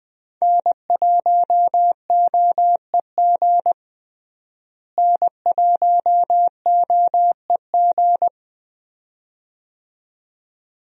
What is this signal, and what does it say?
Morse code, heard as a keyed tone, about 21 wpm, 700 Hz. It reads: N1OEG N1OEG
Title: USA Call Signs - 5 Characters - Repeated 2x - 20wpm